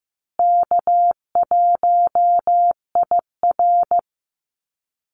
Morse code, heard as a keyed tone, about 15 wpm, 700 Hz. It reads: K1IR
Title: Call Signs Repeated 3x - 15wpm